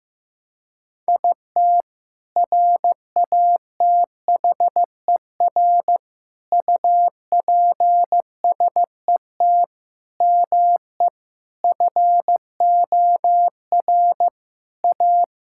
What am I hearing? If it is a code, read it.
Morse code, heard as a keyed tone, about 16 wpm, 700 Hz. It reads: IT RATHER UPSET ME FOR A
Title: Sentences from Top 1000 Words - Review Part 5 - 15wpm